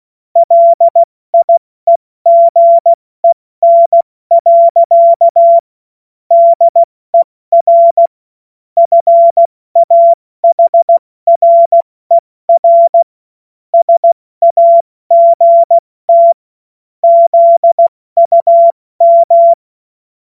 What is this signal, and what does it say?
Morse code, heard as a keyed tone, about 16 wpm, spaced as for 14 wpm, 680 Hz. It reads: LIEGEN. DER FAHRER SAGT ZUM
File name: Morsekurs / Witze / Witz 31. Mai 2022 70 F80